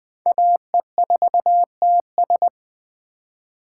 Morse code, heard as a keyed tone, about 20 wpm, 700 Hz. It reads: AE4TS
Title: USA Call Signs - 5 Characters - 2x Character Spacing - 20wpm